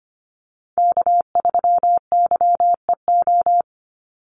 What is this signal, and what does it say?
Morse code, heard as a keyed tone, about 25 wpm, 700 Hz. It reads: K3YEO
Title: Call Signs Repeated 2x - 25wpm